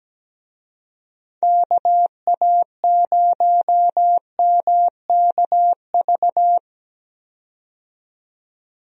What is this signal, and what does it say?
Morse code, heard as a keyed tone, about 17 wpm, 700 Hz. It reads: KA0MKV
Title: USA Call Signs - 6 Characters - 17wpm